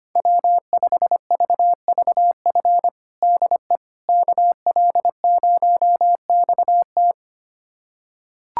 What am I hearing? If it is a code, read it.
Morse code, heard as a keyed tone, about 25 wpm, 700 Hz. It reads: W5VVF DE KL0XT
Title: USA Callsign DE Callsign - 5 Characters - 25wpm